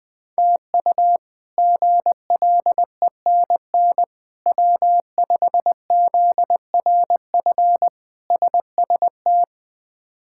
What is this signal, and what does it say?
Morse code, heard as a keyed tone, about 20 wpm, 700 Hz. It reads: TU GLENN W5ZRF SST
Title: K1USN SST QSO Practice - 20wpm